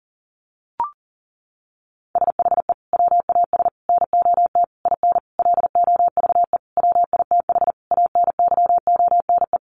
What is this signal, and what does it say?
Morse code, heard as a keyed tone, about 40 wpm, 700 Hz. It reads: SHE WAS NOT IN LOVE WITH ANYONE
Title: Sentences from Top 900 Words 40wpm